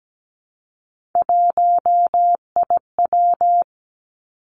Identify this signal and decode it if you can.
Morse code, heard as a keyed tone, about 17 wpm, 700 Hz. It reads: 1IW
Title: Three Challenging Letter-Number Combinations 17wpm